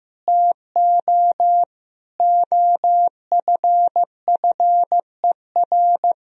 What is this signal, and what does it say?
Morse code, heard as a keyed tone, about 15 wpm, 700 Hz. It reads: TO OFFER
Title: Sentences from Top 1000 Words - Review Part 4 - 15wpm